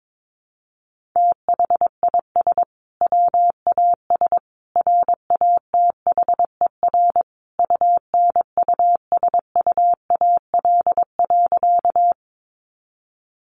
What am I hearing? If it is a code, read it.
Morse code, heard as a keyed tone, about 22 wpm, 700 Hz. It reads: THIS WAS RATHER UNUSUAL.
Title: Sentences from Top 1400 Words 22wpm